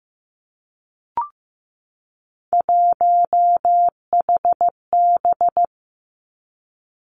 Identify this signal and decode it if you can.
Morse code, heard as a keyed tone, about 15 wpm, 700 Hz. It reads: 1HB
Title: Three Challenging Letter-Number Combinations 15wpm